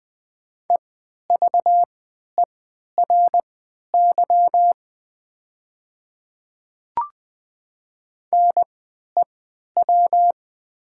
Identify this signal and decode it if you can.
Morse code, heard as a keyed tone, about 20 wpm, spaced as for 11 wpm, 690 Hz. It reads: EVERY NEW
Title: 3x Character Spacing - Top 200 Words - 20wpm